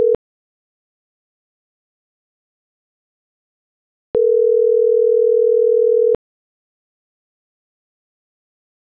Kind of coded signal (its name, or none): telephone call progress tone